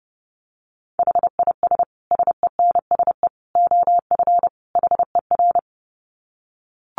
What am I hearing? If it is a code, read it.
Morse code, heard as a keyed tone, about 30 wpm, 700 Hz. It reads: HIS SENSE OF HER
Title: Dit Challenge 30wpm